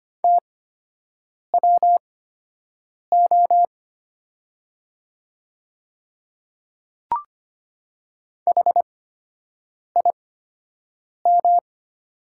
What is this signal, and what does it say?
Morse code, heard as a keyed tone, about 25 wpm, spaced as for 7 wpm, 700 Hz. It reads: TWO HIM
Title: 8x Character Spacing - Top 100 Words - 25wpm